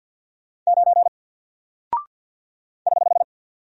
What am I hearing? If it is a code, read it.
Morse code, heard as a keyed tone, about 50 wpm, 700 Hz. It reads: 9 <HH>
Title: Single Character (Letter:Number:Punctuation:Prosign) 50wpm